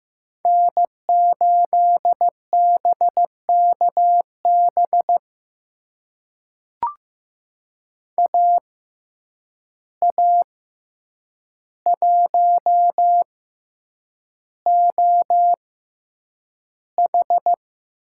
Morse code, heard as a keyed tone, about 15 wpm, spaced as for 8 wpm, 700 Hz. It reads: N8BKB AA1OH
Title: USA Call Signs - 5 Characters - 6x Character Spacing - 15wpm